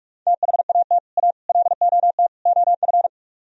Morse code, heard as a keyed tone, about 46 wpm, 700 Hz. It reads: THAT A LOT OF